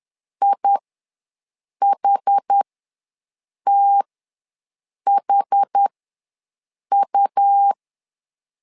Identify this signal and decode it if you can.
Morse code, heard as a keyed tone, about 10 wpm, spaced as for 6 wpm, 800 Hz. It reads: IHTHU
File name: LEZIONI / 2 UHDN / 2 UH ws 10wpm 2029